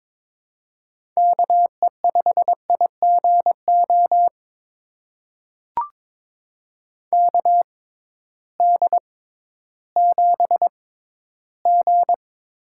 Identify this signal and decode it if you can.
Morse code, heard as a keyed tone, about 22 wpm, spaced as for 13 wpm, 700 Hz. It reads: KE5IGO KD7G
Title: USA Call Signs - 6 Characters - 6x Character Spacing - 22wpm